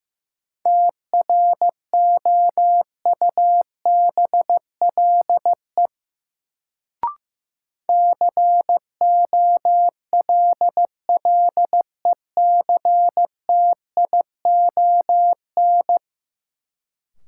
Morse code, heard as a keyed tone, about 15 wpm, 700 Hz. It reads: TROUBLE COLLECTION
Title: Sentences from Top 1200 Words - New Word Drill - 15wpm